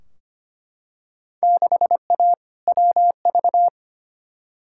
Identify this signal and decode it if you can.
Morse code, heard as a keyed tone, about 25 wpm, 700 Hz. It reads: 6A WV
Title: Field Day Exchanges - 25wpm